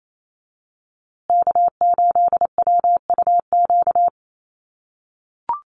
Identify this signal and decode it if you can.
Morse code, heard as a keyed tone, about 28 wpm, 700 Hz. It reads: K8WUQ
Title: USA Call Signs Repeated 3x 28wpm